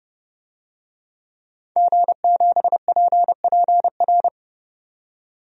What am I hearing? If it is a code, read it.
Morse code, heard as a keyed tone, about 32 wpm, 700 Hz. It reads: G7PPR
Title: DXCC-100 Call Signs - 5 Characters - 30wpm